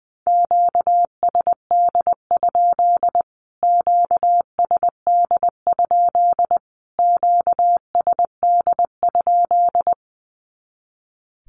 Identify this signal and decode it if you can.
Morse code, heard as a keyed tone, about 20 wpm, 700 Hz. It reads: QSD? QSD? QSD?
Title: Q-Codes Repeated 3x 20wpm